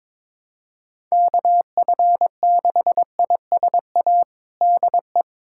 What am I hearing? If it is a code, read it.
Morse code, heard as a keyed tone, about 22 wpm, 700 Hz. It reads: KF6ISA DE
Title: USA Callsign DE Callsign - 6 Characters - 22wpm